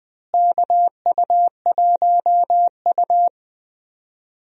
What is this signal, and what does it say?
Morse code, heard as a keyed tone, about 20 wpm, 700 Hz. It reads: KU1U
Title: USA Call Signs - 4 Characters - Repeated 2x - 20wpm